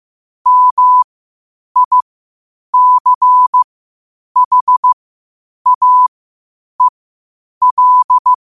Morse code, heard as a keyed tone, about 15 wpm, spaced as for 8 wpm, 1000 Hz. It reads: MICHAEL
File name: qotdfiles / 8wpm